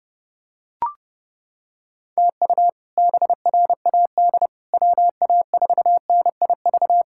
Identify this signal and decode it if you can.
Morse code, heard as a keyed tone, about 32 wpm, 700 Hz. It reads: TU BRAD WA4NIV
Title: K1USN SST QSO Practice - 30wpm